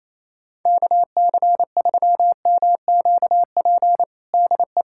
Morse code, heard as a keyed tone, about 28 wpm, 700 Hz. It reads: KC3MQP DE
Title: USA Callsign DE Callsign 28wpm